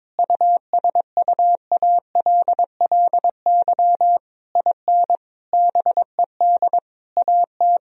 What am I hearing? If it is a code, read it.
Morse code, heard as a keyed tone, about 22 wpm, 700 Hz. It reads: USUALLY IN BED AT
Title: Sentences from Top 1000 Words - Review Part 3 - 22wpm